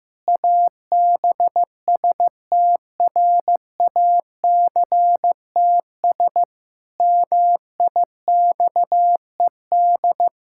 Morse code, heard as a keyed tone, about 16 wpm, 700 Hz. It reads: ABSTRACTS MIXED
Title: Pangram Challenge v2 - 15wpm